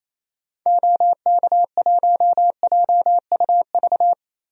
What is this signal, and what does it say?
Morse code, heard as a keyed tone, about 28 wpm, 700 Hz. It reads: OK1JUV
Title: DXCC-100 Call Signs - 6 Characters - 28wpm